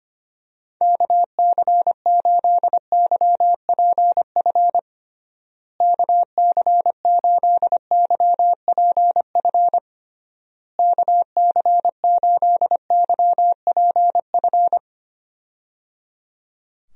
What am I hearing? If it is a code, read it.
Morse code, heard as a keyed tone, about 25 wpm, 700 Hz. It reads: KC8YPF KC8YPF KC8YPF
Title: USA Call Signs - 6 Characters - Repeated 3x - 25wpm